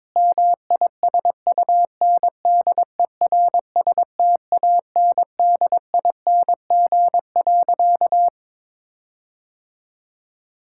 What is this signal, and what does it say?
Morse code, heard as a keyed tone, about 22 wpm, 700 Hz. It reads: MISUNDERSTANDING.
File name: practice / 22wpm / Sentences with 4-Word Phrases - 22wpm